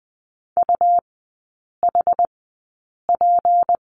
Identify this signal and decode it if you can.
Morse code, heard as a keyed tone, about 20 wpm, spaced as for 8 wpm, 700 Hz. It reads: UHP
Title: 218 Lesson 35e - Number 0 - Three Characters 20wpm